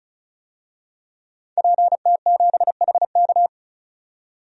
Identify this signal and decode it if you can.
Morse code, heard as a keyed tone, about 34 wpm, 700 Hz. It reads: PT7HK
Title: DXCC-100 Call Signs - 5 Characters - 35wpm